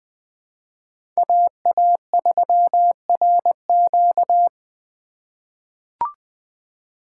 Morse code, heard as a keyed tone, about 20 wpm, 700 Hz. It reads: AA3RQ
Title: USA Call Signs - 5 Characters - 3x Character Spacing - 20wpm